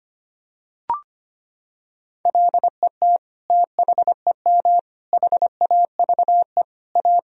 Morse code, heard as a keyed tone, about 25 wpm, 700 Hz. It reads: LET THEM HAVE A
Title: Sentences from Top 1000 Words - Review Part 3 - 25wpm